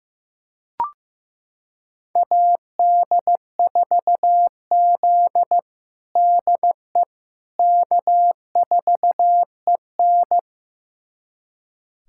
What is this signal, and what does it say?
Morse code, heard as a keyed tone, about 15 wpm, 700 Hz. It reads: AD4Z DE K4EN
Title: USA Callsign DE Callsign - 4 Characters - 15wpm